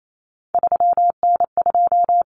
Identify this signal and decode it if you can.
Morse code, heard as a keyed tone, about 28 wpm, 700 Hz. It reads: 3N2
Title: Common Cut Numbers - 3-Digit 28wpm